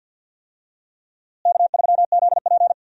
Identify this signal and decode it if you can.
Morse code, heard as a keyed tone, about 50 wpm, 700 Hz. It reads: K3ZP
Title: USA Call Signs - 4 Characters - Repeated 2x - 50wpm